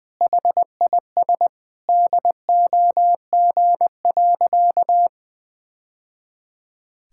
Morse code, heard as a keyed tone, about 20 wpm, 700 Hz. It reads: HIS DOG.
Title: Sentences with 2-Word Phrases - 20wpm